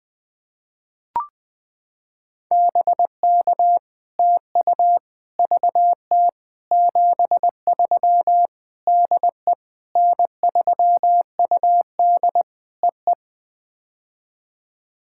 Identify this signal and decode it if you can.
Morse code, heard as a keyed tone, about 20 wpm, 700 Hz. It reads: BK TU VT 73 DE N3UD EE
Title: POTA QSO Practice - 20wpm